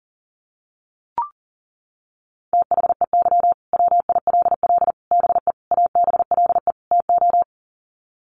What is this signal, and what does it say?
Morse code, heard as a keyed tone, about 40 wpm, 700 Hz. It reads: THEY WILL BE ABLE TO
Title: Sets of 5 Words Encore 40wpm